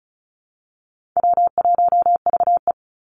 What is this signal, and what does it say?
Morse code, heard as a keyed tone, about 34 wpm, 700 Hz. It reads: W1VE
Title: Easy Call Signs Repeated 2x - 35wpm